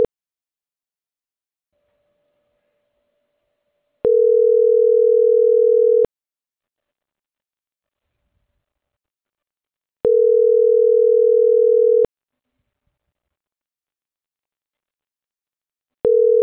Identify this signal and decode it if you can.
Ringback tone (North America), 440 + 480 Hz.